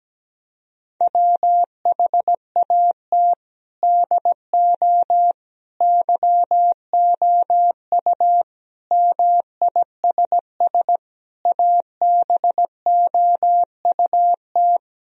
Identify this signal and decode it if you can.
Morse code, heard as a keyed tone, about 17 wpm, 700 Hz. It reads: WHAT DO YOU MISS ABOUT
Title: Sentences from Top 1000 Words - Review Part 3 - 17wpm